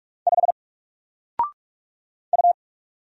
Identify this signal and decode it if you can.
Morse code, heard as a keyed tone, about 44 wpm, 700 Hz. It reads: 5 U
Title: Single Letter-Number 45wpm